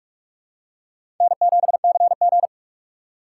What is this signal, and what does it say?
Morse code, heard as a keyed tone, about 46 wpm, 700 Hz. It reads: N7CG
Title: USA Call Signs - 6x Character Spacing - 45wpm